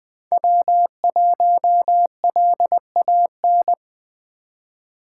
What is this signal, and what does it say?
Morse code, heard as a keyed tone, about 21 wpm, 700 Hz. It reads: W1LAN
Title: USA Call Signs - 5 Characters - 3x Character Spacing - 20wpm